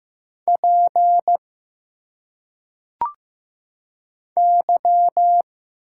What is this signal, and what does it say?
Morse code, heard as a keyed tone, about 15 wpm, 700 Hz. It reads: P Y
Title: Single Letters - Phonetic Pronunciation 15wpm